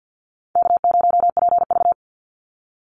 Morse code, heard as a keyed tone, about 50 wpm, 700 Hz. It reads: K0PV
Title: Easy Call Signs - 50wpm